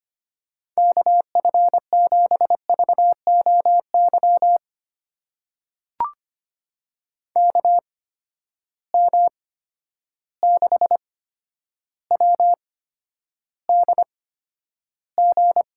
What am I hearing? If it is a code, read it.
Morse code, heard as a keyed tone, about 25 wpm, spaced as for 11 wpm, 700 Hz. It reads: KF7VOY KM6WDG
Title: USA Call Signs - 6 Characters - 8x Character Spacing - 25wpm